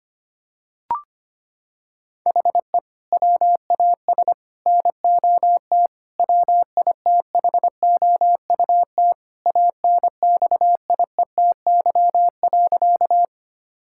Morse code, heard as a keyed tone, about 25 wpm, 700 Hz. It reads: HE WAS NOT WITHOUT ANXIETY.
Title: Sentences from Top 1600 Words 25wpm